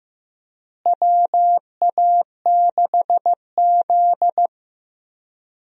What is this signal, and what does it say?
Morse code, heard as a keyed tone, about 15 wpm, 700 Hz. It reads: WA6Z
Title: USA Call Signs - 4 Characters - Repeated 3x - 15wpm